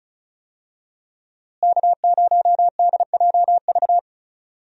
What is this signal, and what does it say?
Morse code, heard as a keyed tone, about 36 wpm, 700 Hz. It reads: K0DJV
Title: USA Call Signs - 5 Characters - Repeated 2x - 35wpm